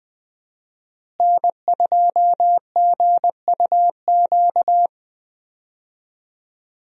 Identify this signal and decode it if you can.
Morse code, heard as a keyed tone, about 20 wpm, 700 Hz. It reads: N2GUQ
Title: USA Call Signs - 5 Characters - Repeated 2x - 20wpm